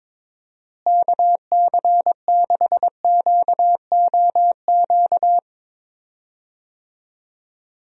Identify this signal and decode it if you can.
Morse code, heard as a keyed tone, about 22 wpm, 700 Hz. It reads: KC6QOQ
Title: USA Call Signs - 6 Characters - Repeated 3x - 22wpm